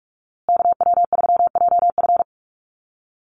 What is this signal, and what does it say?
Morse code, heard as a keyed tone, about 44 wpm, 700 Hz. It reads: KW3JF
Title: USA Call Signs Repeated 3x 45wpm